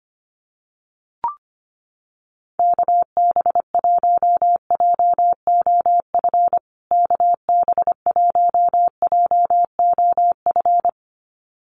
Morse code, heard as a keyed tone, about 25 wpm, 700 Hz. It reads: KB1JOF KB1JOF
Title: USA Call Signs Repeated 2x 25wpm